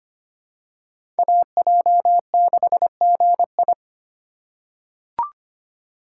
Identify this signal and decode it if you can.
Morse code, heard as a keyed tone, about 25 wpm, 700 Hz. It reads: AJ6GI
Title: USA Call Signs - 4x Character Spacing - 25wpm